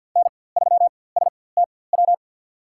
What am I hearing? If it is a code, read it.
Morse code, heard as a keyed tone, about 46 wpm, spaced as for 23 wpm, 700 Hz. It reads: N3STW
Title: USA Call Signs - 5 Characters - 4x Character Spacing - 50wpm